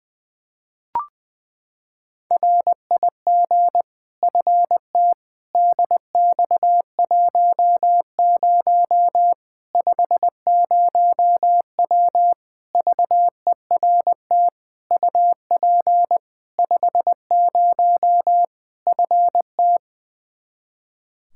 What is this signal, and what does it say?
Morse code, heard as a keyed tone, about 21 wpm, 700 Hz. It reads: RIG FT DX10 50W VERT UP 50 FT
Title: Practice QSO - Exchange 2 - Info - 20wpm